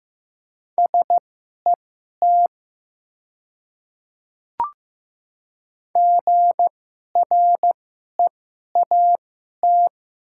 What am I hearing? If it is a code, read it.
Morse code, heard as a keyed tone, about 15 wpm, spaced as for 11 wpm, 700 Hz. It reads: SET GREAT
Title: 2x Character Spacing - Top 200 Words - 15wpm